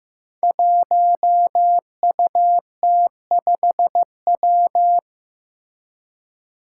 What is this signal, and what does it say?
Morse code, heard as a keyed tone, about 16 wpm, 700 Hz. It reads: 1UT5W
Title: Five Challenging Letter-Number Combinations 15wpm